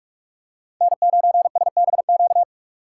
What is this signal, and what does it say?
Morse code, heard as a keyed tone, about 46 wpm, 700 Hz. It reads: N9SBQ
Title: USA Call Signs - 5 Characters - 4x Character Spacing - 45wpm